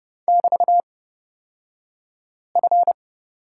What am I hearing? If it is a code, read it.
Morse code, heard as a keyed tone, about 30 wpm, spaced as for 5 wpm, 700 Hz. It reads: <BT>F
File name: Lesson 37d - Letter x - Two Characters 30wpm